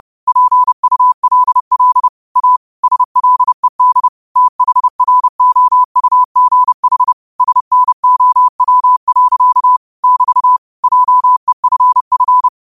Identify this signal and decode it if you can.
Morse code, heard as a keyed tone, about 30 wpm, 1000 Hz. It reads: PULL A SLED THROUGH SNOW. <BT> JEFF